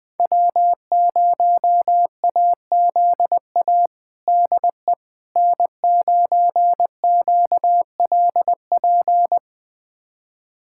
Morse code, heard as a keyed tone, about 21 wpm, 700 Hz. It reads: W0AZA DE N9QLP
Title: USA Callsign DE Callsign - 5 Characters - 20wpm